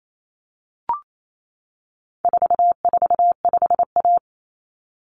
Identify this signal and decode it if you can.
Morse code, heard as a keyed tone, about 28 wpm, 700 Hz. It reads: 445A
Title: Common Cut Numbers - 4-Digit 28wpm